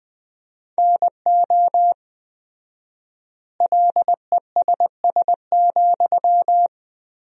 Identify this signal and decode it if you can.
Morse code, heard as a keyed tone, about 20 wpm, 700 Hz. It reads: NO LESS,
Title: 4x Word-Spacing - Sentences from Top 400 Words - 20wpm